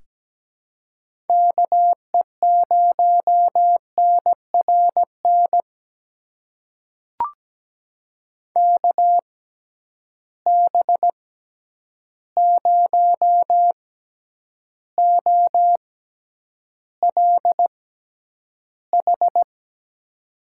Morse code, heard as a keyed tone, about 17 wpm, spaced as for 9 wpm, 700 Hz. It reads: KE0NRN KB0OLH
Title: USA Call Signs - 6 Characters - 6x Character Spacing - 17wpm